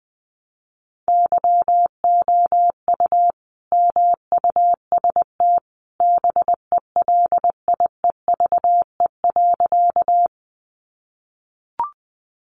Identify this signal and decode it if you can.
Morse code, heard as a keyed tone, about 20 wpm, 700 Hz. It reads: YOU MUST BELIEVE.
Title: Sentences from Top 1000 Words 20wpm